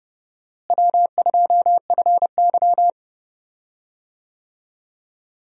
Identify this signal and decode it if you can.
Morse code, heard as a keyed tone, about 30 wpm, 700 Hz. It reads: W2FY
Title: USA Call Signs - 4 Characters - Repeated 3x - 30wpm